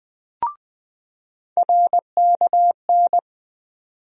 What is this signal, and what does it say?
Morse code, heard as a keyed tone, about 20 wpm, 700 Hz. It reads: RKN
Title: USA Call Sign Postfixes - 3 Letters 20wpm